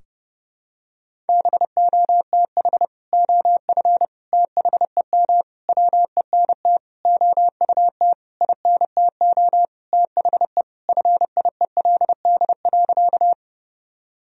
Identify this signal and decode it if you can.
Morse code, heard as a keyed tone, about 30 wpm, 700 Hz. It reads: BOTH OF THEM WENT OUT INTO THE FIELD.
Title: Sentences from Top 1000 Words - Review Part 2 - 30wpm